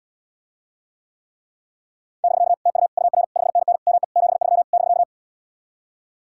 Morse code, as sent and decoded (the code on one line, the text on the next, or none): -...-.- / - ..- / ..- .-. / ..... -. -. / -- . / --... ...-- / -...-.-
<BK> TU UR 5NN ME 73 <BK>